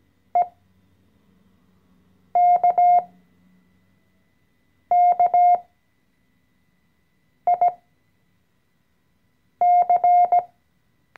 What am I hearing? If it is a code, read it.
Morse code, heard as a keyed tone, about 17 wpm, spaced as for 4 wpm, 710 Hz. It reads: EKKIC
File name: CW-Online-Kurs / Anfänger / Lektion 014